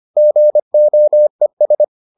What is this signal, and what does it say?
Morse code, heard as a keyed tone, about 25 wpm, 600 Hz. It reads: GOES